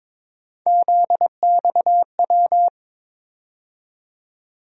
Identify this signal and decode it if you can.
Morse code, heard as a keyed tone, about 22 wpm, 700 Hz. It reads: ZXW
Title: USA Call Sign Postfixes - 3 Letters 22wpm